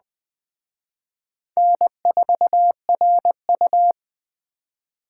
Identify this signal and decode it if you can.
Morse code, heard as a keyed tone, about 20 wpm, 700 Hz. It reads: N4RU